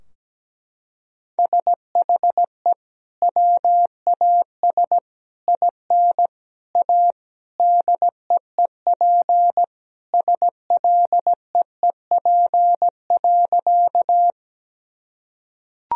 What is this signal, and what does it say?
Morse code, heard as a keyed tone, about 17 wpm, 700 Hz. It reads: SHE WAS IN A DEEP SLEEP.
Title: Sentences from Top 1000 Words - Review Part 3 - 17wpm